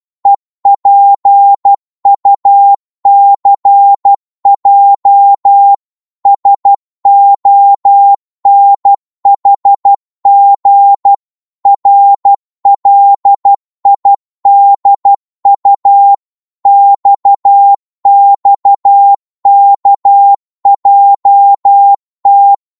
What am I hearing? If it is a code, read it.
Morse code, heard as a keyed tone, about 12 wpm, 800 Hz. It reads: EPUCJ SONHG RLIDU XXKJT
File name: Morse / ltr-12wpm